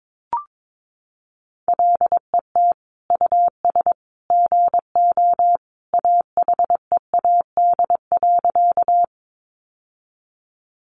Morse code, heard as a keyed tone, about 22 wpm, 700 Hz. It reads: LET US GO AHEAD.
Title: Sentences from Top 1100 Words 22wpm